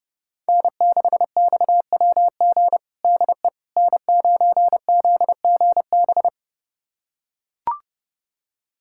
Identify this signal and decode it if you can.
Morse code, heard as a keyed tone, about 30 wpm, 700 Hz. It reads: N6XWG DE N9ZGB
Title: USA Callsign DE Callsign - 5 Characters - 30wpm